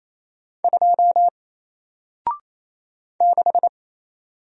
Morse code, heard as a keyed tone, about 29 wpm, 700 Hz. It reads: 2 6
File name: Single-Letter-Number-28wpm